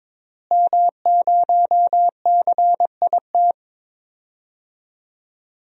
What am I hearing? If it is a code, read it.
Morse code, heard as a keyed tone, about 21 wpm, 700 Hz. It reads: M0CIT
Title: DXCC-100 Call Signs - 5 Characters - Repeated 3x - 22wpm